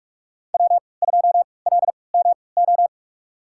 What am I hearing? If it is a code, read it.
Morse code, heard as a keyed tone, about 47 wpm, spaced as for 26 wpm, 700 Hz. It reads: W2LMO
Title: USA Call Signs - 5 Characters - 3x Character Spacing - 45wpm